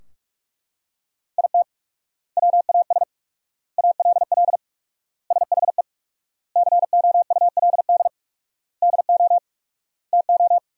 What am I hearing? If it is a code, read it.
Morse code, heard as a keyed tone, about 43 wpm, 700 Hz. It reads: IT WAS ALL SHE COULD DO TO
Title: 4x Word-Spacing - Sentences from Top 400 Words - 45wpm